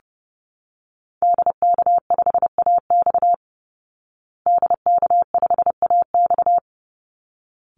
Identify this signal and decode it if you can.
Morse code, heard as a keyed tone, about 30 wpm, 700 Hz. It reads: DK5AX DK5AX
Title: Call Signs Repeated 3x - 30wpm